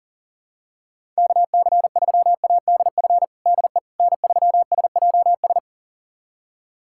Morse code, heard as a keyed tone, about 40 wpm, 700 Hz. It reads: KC3ADF DE N3SJS
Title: USA Callsign DE Callsign 40wpm